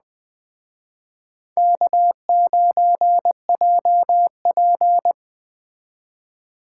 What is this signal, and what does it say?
Morse code, heard as a keyed tone, about 20 wpm, 700 Hz. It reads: K9JP
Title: Speed Racing - USA Call Signs 17wpm